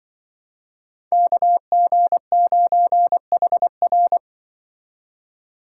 Morse code, heard as a keyed tone, about 24 wpm, 700 Hz. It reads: KG9HR
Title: Speed Racing - USA 2x2 Call Signs 20wpm